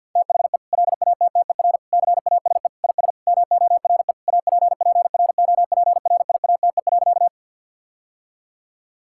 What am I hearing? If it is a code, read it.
Morse code, heard as a keyed tone, about 50 wpm, 700 Hz. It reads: THE LATTER CASE IS MORE APPROPRIATE.
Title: Sentences from Top 1600 Words 50wpm